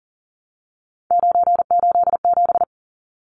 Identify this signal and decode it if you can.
Morse code, heard as a keyed tone, about 40 wpm, 700 Hz. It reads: 987
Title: Numbers - 3-Digit 40wpm